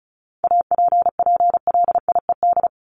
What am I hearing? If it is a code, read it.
Morse code, heard as a keyed tone, about 34 wpm, 700 Hz. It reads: APPLIED
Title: ED Suffix - Top 100 Words with Base Words 35wpm